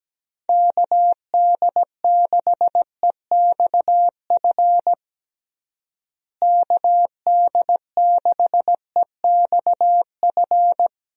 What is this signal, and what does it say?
Morse code, heard as a keyed tone, about 17 wpm, 700 Hz. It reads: KD6EXF KD6EXF
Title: USA Call Signs - 6 Characters - Repeated 3x - 17wpm